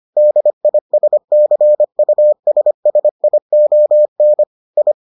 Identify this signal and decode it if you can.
Morse code, heard as a keyed tone, about 25 wpm, 600 Hz. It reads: DISCUSSION I